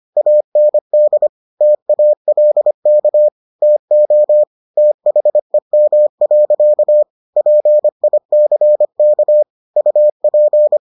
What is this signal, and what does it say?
Morse code, heard as a keyed tone, about 25 wpm, 600 Hz. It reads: AND TALK TO THEM. PICK UP